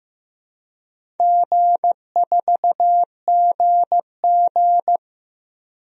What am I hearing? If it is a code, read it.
Morse code, heard as a keyed tone, about 15 wpm, 700 Hz. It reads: G4GG
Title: DXCC-100 Call Signs - 4 Characters - 15wpm